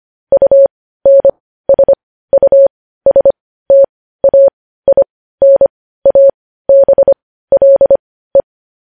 Morse code, heard as a keyed tone, about 25 wpm, spaced as for 15 wpm, 550 Hz. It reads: UNSUSTAINABLE
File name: podcast / News headlines-1x-25x15